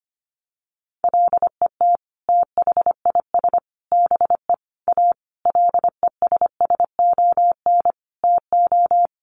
Morse code, heard as a keyed tone, about 25 wpm, 700 Hz. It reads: LET THIS BE A LESSON TO
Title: Sentences from Top 1200 Words 25wpm